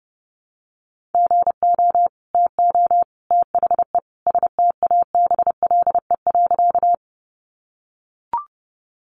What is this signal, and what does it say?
Morse code, heard as a keyed tone, about 30 wpm, 700 Hz. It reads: GO TO THE STABLE.
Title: Sentences from Top 1500 Words 30wpm